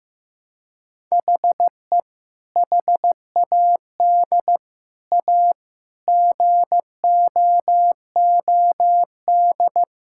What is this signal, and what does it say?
Morse code, heard as a keyed tone, about 15 wpm, 700 Hz. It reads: HE HAD A GOOD
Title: Sentences from Top 1000 Words - Review Part 6 - 15wpm